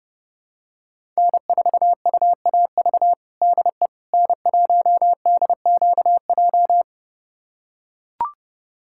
Morse code, heard as a keyed tone, about 30 wpm, 700 Hz. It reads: N4UAV DE N1DQJ
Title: USA Callsign DE Callsign - 5 Characters - 30wpm